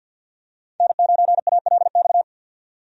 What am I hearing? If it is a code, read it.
Morse code, heard as a keyed tone, about 52 wpm, 700 Hz. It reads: N9ALX
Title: USA Call Signs - 5 Characters - 50wpm